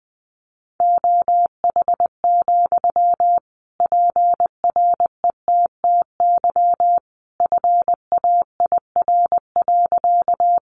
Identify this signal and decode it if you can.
Morse code, heard as a keyed tone, about 20 wpm, 700 Hz. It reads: OH, PRETTY FAIR.
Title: Sentences from Top 1300 Words 20wpm